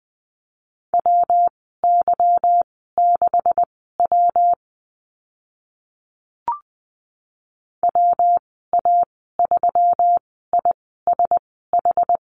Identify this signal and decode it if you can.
Morse code, heard as a keyed tone, about 19 wpm, spaced as for 14 wpm, 700 Hz. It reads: WY6W WA3ISH
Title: 167 Lesson 27g - Number 7 - Call Signs 20wpm